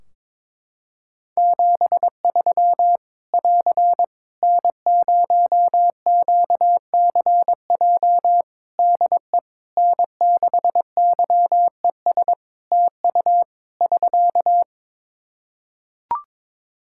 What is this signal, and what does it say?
Morse code, heard as a keyed tone, about 21 wpm, 700 Hz. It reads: 73 <AR> N0QCJ DE N6YES TU <SK>
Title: K1USN SST QSO Practice 22wpm